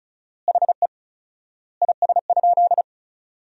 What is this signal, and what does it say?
Morse code, heard as a keyed tone, about 36 wpm, 700 Hz. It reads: HE IS?
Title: 4x Word-Spacing - Sentences from Top 300 Words - 35wpm